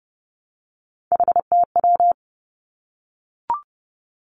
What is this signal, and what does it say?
Morse code, heard as a keyed tone, about 30 wpm, 700 Hz. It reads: HTW
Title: Three Challenging Letter-Number Combinations 30wpm